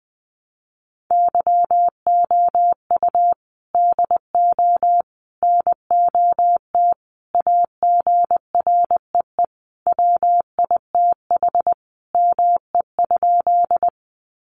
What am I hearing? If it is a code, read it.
Morse code, heard as a keyed tone, about 20 wpm, 700 Hz. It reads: YOU DO NOT AGREE WITH ME?
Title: Sentences from Top 1100 Words 20wpm